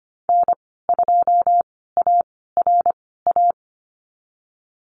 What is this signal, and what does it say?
Morse code, heard as a keyed tone, about 25 wpm, spaced as for 16 wpm, 700 Hz. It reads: N2ARA
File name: Lesson 15g - Number 5 - Call Signs 25wpm